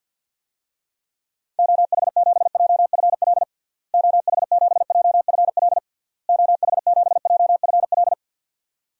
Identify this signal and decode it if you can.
Morse code, heard as a keyed tone, about 52 wpm, 700 Hz. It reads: OH7JFL OH7JFL OH7JFL
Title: DXCC-100 Call Signs - 6 Characters - Repeated 3x - 50wpm